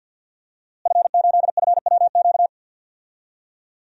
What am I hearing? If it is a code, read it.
Morse code, heard as a keyed tone, about 52 wpm, 700 Hz. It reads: F8FWQ
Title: DXCC-100 Call Signs - 5 Characters - Repeated 3x - 50wpm